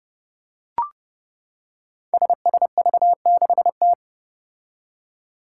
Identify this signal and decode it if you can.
Morse code, heard as a keyed tone, about 31 wpm, 700 Hz. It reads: SSV6T
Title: Five Challenging Letter-Number Combinations 30wpm